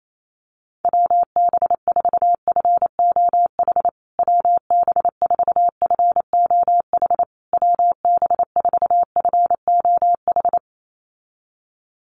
Morse code, heard as a keyed tone, about 28 wpm, 700 Hz. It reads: WB4FOH WB4FOH WB4FOH
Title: USA Call Signs Repeated 3x 28wpm